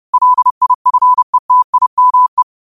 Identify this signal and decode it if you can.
Morse code, heard as a keyed tone, about 30 wpm, 1000 Hz. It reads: LIFETIME
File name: qotdfiles / 30wpm